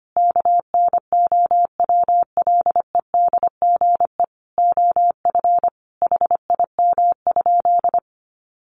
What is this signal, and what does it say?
Morse code, heard as a keyed tone, about 25 wpm, 700 Hz. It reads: KNOWLEDGE OF HIM?
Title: Sentences from Top 600 Words 25wpm